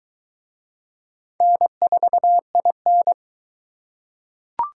Morse code, heard as a keyed tone, about 23 wpm, 700 Hz. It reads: N4IN